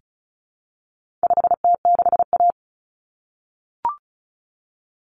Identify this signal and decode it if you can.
Morse code, heard as a keyed tone, about 34 wpm, 700 Hz. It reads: 5T6A
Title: Common Cut Numbers - 4-Digit 35wpm